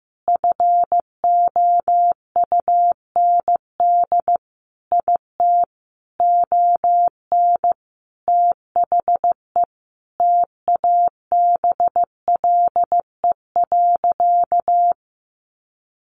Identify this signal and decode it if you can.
Morse code, heard as a keyed tone, about 15 wpm, 700 Hz. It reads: FOUND IT ON THE TABLE.
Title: Sentences from Top 1500 Words 15wpm